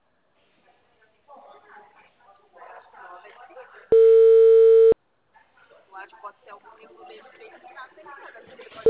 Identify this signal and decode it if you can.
Ringback tone (China, 450 Hz), with speech.